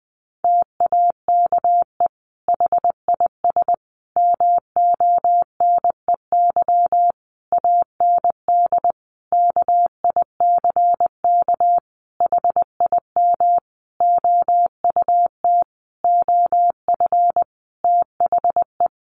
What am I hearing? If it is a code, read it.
Morse code, heard as a keyed tone, about 20 wpm, 700 Hz. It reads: TAKE HIS MONEY AND KICK HIM OUT OF THE
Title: Sentences from Top 1300 Words 20wpm